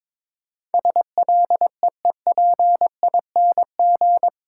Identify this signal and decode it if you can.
Morse code, heard as a keyed tone, about 22 wpm, 700 Hz. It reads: SLEEPING
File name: Difficult Words with Hints - 22wpm